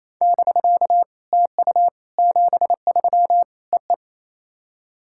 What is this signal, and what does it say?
Morse code, heard as a keyed tone, about 27 wpm, 690 Hz. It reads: <BK> TU 73 EE
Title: SOTA QSO Practice 28wpm